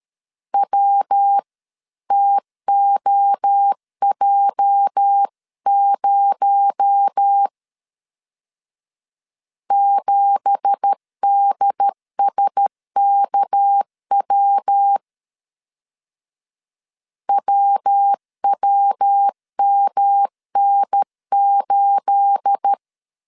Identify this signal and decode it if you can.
Morse code, heard as a keyed tone, about 13 wpm, 800 Hz. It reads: WTOJ0 7DSKW WWMN8
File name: CW / DET / 1 9 1500C S B 12wpm 3819